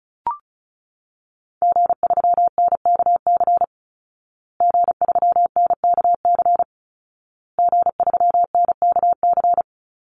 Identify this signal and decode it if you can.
Morse code, heard as a keyed tone, about 34 wpm, 700 Hz. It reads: G3NKC G3NKC G3NKC
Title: Call Signs Repeated 2x - 35wpm